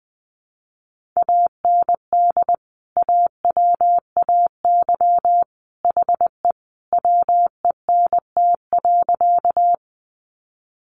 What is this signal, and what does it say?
Morse code, heard as a keyed tone, about 20 wpm, 700 Hz. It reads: AND AWAY HE WENT.
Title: Sentences from Top 600 Words 20wpm